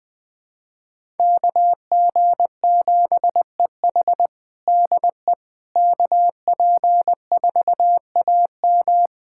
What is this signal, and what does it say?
Morse code, heard as a keyed tone, about 20 wpm, 700 Hz. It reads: KG7EH DE KP4AM
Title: USA Callsign DE Callsign - 5 Characters - 20wpm